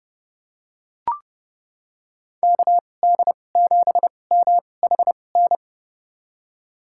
Morse code, heard as a keyed tone, about 30 wpm, spaced as for 22 wpm, 700 Hz. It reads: KD7MHN
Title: USA Call Signs - 2x Character Spacing - 30wpm